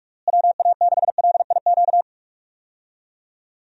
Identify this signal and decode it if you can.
Morse code, heard as a keyed tone, about 47 wpm, 700 Hz. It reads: WA6LIQ
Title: USA Call Signs - 6 Characters - Repeated 2x - 45wpm